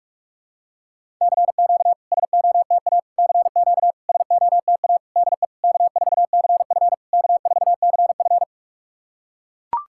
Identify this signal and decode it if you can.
Morse code, heard as a keyed tone, about 44 wpm, 690 Hz. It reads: CQ SOTA CQ SOTA DE K4CF K4CF
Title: SOTA QSO Practice 45wpm